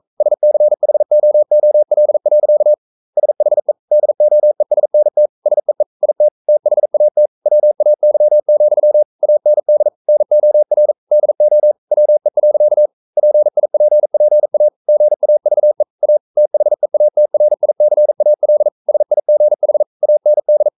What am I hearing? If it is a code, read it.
Morse code, heard as a keyed tone, about 42 wpm, 600 Hz. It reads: SCHOOL. SHE DOESNT SEE IT THAT WAY, AND NOR DO WE. PIPPA GAVE A THEATRICAL SIGH AND
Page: Romanen Never af Ken Follet på engelsk.